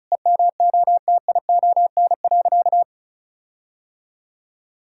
Morse code, heard as a keyed tone, about 36 wpm, 700 Hz. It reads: EMOTION.
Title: Sentences from Top 1600 Words 35wpm